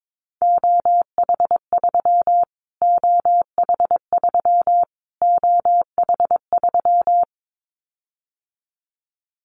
Morse code, heard as a keyed tone, about 22 wpm, 700 Hz. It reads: OH3 OH3 OH3
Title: Callsign Prefixes for Top 25 DXCC - Repeated 3x - 22wpm